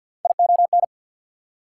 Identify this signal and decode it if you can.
Morse code, heard as a keyed tone, about 48 wpm, 690 Hz. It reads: ION
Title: Prefixes and Suffixes 50wpm